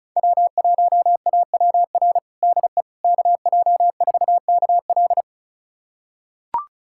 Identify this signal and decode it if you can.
Morse code, heard as a keyed tone, about 34 wpm, 700 Hz. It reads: W1AWR DE KJ4KL
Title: USA Callsign DE Callsign - 5 Characters - 35wpm